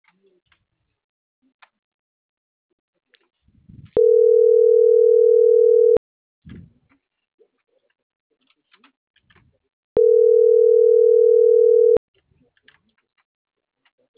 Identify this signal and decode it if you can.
Ringback tone (North America), 440 + 480 Hz.